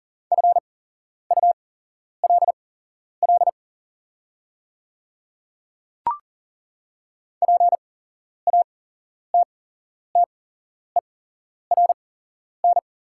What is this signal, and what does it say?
Morse code, heard as a keyed tone, about 40 wpm, spaced as for 11 wpm, 690 Hz. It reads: FULL PATTERN
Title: 8x Character Spacing - Top 500 Words - 40wpm